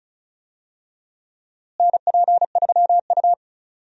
Morse code, heard as a keyed tone, about 36 wpm, 700 Hz. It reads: NP3U
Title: USA Call Signs - 4 Characters - Repeated 2x - 35wpm